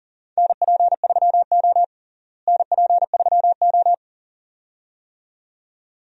Morse code, heard as a keyed tone, about 40 wpm, 700 Hz. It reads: NP3O NP3O
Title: USA Call Signs - 4 Characters - Repeated 2x - 40wpm